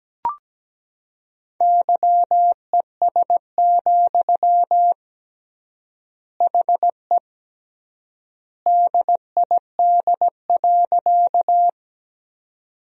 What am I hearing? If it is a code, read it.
Morse code, heard as a keyed tone, about 17 wpm, 700 Hz. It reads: YES, HE DID.
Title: 3x Word-Spacing - Sentences from Top 500 Words - 17wpm